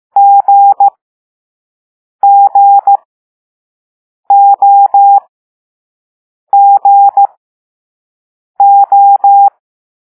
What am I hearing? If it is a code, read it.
Morse code, heard as a keyed tone, about 15 wpm, spaced as for 5 wpm, 800 Hz. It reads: GGOGO
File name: Morse / Level4B